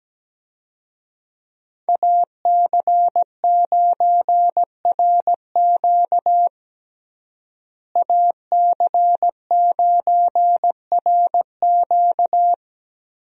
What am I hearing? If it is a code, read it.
Morse code, heard as a keyed tone, about 17 wpm, 700 Hz. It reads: AC9RQ AC9RQ
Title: USA Call Signs - 5 Characters - Repeated 2x - 17wpm